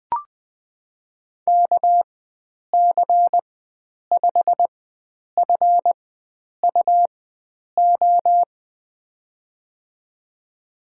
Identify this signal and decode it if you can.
Morse code, heard as a keyed tone, about 20 wpm, spaced as for 9 wpm, 700 Hz. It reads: KC5FUO
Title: USA Call Signs - 4x Character Spacing - 20wpm